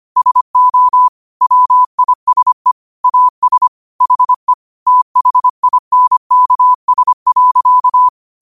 Morse code, heard as a keyed tone, about 25 wpm, 1000 Hz. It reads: SO WISE AS HE THINKS.